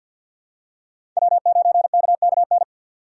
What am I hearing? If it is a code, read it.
Morse code, heard as a keyed tone, about 50 wpm, 700 Hz. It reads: W9KKN
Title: USA Call Signs - 4x Character Spacing - 50wpm